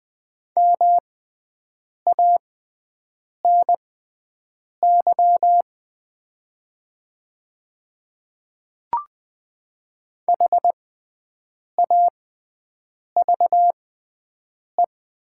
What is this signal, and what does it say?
Morse code, heard as a keyed tone, about 20 wpm, spaced as for 7 wpm, 700 Hz. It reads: MANY HAVE
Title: 6x Character Spacing - Top 100 Words - 20wpm